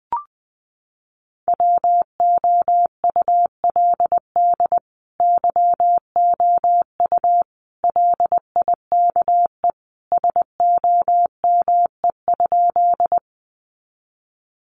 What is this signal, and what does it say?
Morse code, heard as a keyed tone, about 20 wpm, 700 Hz. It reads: WOULD YOU LIKE SOME?
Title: Sentences from Top 100 Words 20wpm